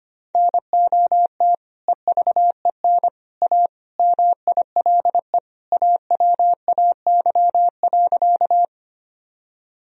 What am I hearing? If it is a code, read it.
Morse code, heard as a keyed tone, about 25 wpm, 700 Hz. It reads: NOT EVEN A MILE AWAY.
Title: Sentences from Top 1000 Words - Review Part 1 - 25wpm